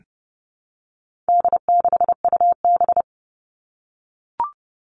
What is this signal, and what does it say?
Morse code, heard as a keyed tone, about 30 wpm, 700 Hz. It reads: D6UB
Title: All Cut Numbers - 4-Digit 30wpm